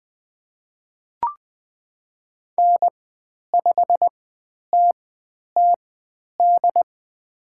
Morse code, heard as a keyed tone, about 20 wpm, spaced as for 10 wpm, 700 Hz. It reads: N5TTD
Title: 095 Lesson 16g - Letter c - Call Signs 20wpm